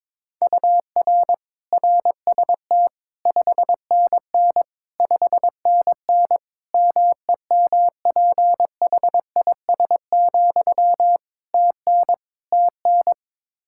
Morse code, heard as a keyed tone, about 22 wpm, 700 Hz. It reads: UR RST 5NN 5NN MEMPHIS, TN TN
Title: Practice QSO - Exchange 1 - Beginning - 22wpm